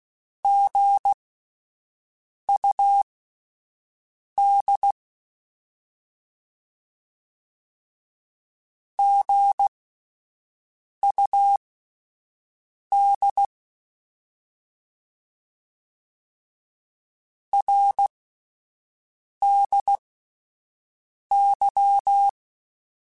Morse code, heard as a keyed tone, about 16 wpm, spaced as for 5 wpm, 800 Hz. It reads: GUD GUD RDY
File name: Abreviations / Abreviations mp3s / Abreviations@5